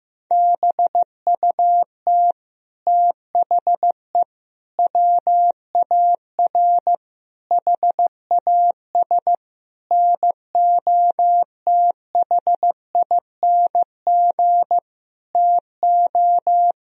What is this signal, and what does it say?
Morse code, heard as a keyed tone, about 15 wpm, 700 Hz. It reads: BUT THE WAR HAS NOTHING TO
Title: Sentences from Top 1000 Words - Review Part 2 - 15wpm